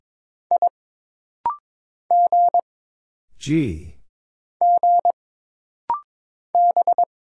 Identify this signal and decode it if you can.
Morse code, heard as a keyed tone, about 22 wpm, 700 Hz. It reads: I G G B
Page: This rapid-fire timing is geared to push your ICR (Instant Character Recognition) skills.